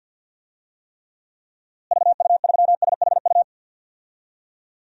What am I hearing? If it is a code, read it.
Morse code, heard as a keyed tone, about 48 wpm, 700 Hz. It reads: VU3SHU
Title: DXCC-100 Call Signs - 6 Characters - Repeated 2x - 50wpm